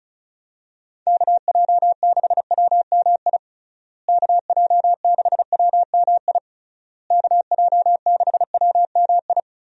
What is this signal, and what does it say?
Morse code, heard as a keyed tone, about 36 wpm, 700 Hz. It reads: KJ6WMI KJ6WMI KJ6WMI
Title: USA Call Signs - 6 Characters - Repeated 3x - 35wpm